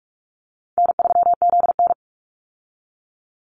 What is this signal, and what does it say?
Morse code, heard as a keyed tone, about 44 wpm, 700 Hz. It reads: N3ZN
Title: Easy Call Signs Repeated 2x - 45wpm